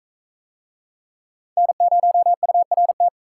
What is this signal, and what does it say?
Morse code, heard as a keyed tone, about 41 wpm, 700 Hz. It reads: N0URT